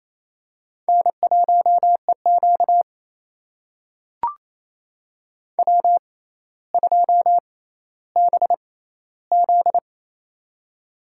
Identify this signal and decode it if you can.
Morse code, heard as a keyed tone, about 28 wpm, spaced as for 14 wpm, 700 Hz. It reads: N1EQ W2BZ
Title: USA Call Signs - 4 Characters - 6x Character Spacing - 28wpm